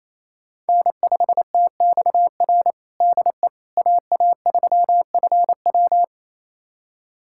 Morse code, heard as a keyed tone, about 27 wpm, 700 Hz. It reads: N5TXR DE AA3FW
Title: USA Callsign DE Callsign - 5 Characters - 28wpm